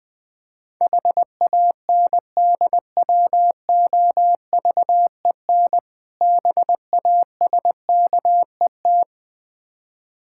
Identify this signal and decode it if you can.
Morse code, heard as a keyed tone, about 20 wpm, 700 Hz. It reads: HANDWOVEN BASKET
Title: Difficult Words with Hints - 20wpm